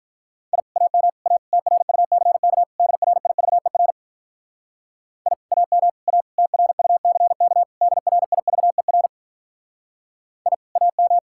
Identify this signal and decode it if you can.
Morse code, heard as a keyed tone, about 52 wpm, 700 Hz. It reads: I AM A TRUCK DRIVER I AM A TRUCK DRIVER I AM